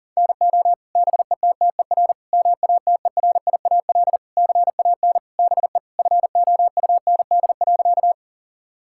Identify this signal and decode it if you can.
Morse code, heard as a keyed tone, about 40 wpm, 700 Hz. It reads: NO BETTER MATERIAL CAN BE FOUND.
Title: Sentences from Top 1000 Words - Review Part 5 - 40wpm